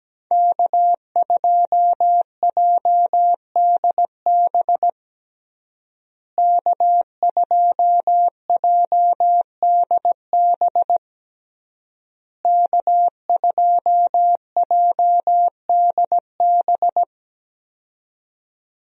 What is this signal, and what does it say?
Morse code, heard as a keyed tone, about 17 wpm, 700 Hz. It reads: K2JDB K2JDB K2JDB
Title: USA Call Signs - 5 Characters - Repeated 3x - 17wpm